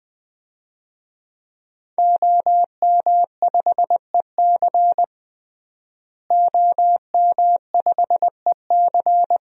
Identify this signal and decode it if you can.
Morse code, heard as a keyed tone, about 20 wpm, 700 Hz. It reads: OM5EC OM5EC
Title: DXCC-100 Call Signs - 5 Characters - Repeated 3x - 20wpm